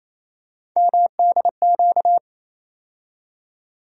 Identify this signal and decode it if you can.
Morse code, heard as a keyed tone, about 28 wpm, 700 Hz. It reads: MDQ
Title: USA Call Sign Postfixes - 3 Letters 28wpm